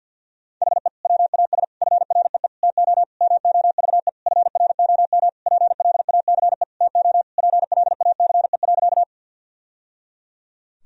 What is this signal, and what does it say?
Morse code, heard as a keyed tone, about 49 wpm, 700 Hz. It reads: HE WAS FREE TO MOVE FROM PLACE TO PLACE.
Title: Sentences from Top 1000 Words - Review Part 1 - 50wpm